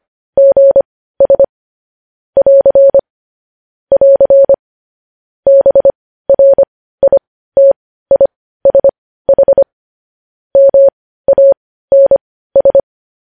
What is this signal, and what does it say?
Morse code, heard as a keyed tone, about 25 wpm, spaced as for 15 wpm, 550 Hz. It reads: GS <AR> <AR> BRITISH MANS
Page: news-headlines-in-morse-code-at-15-wpm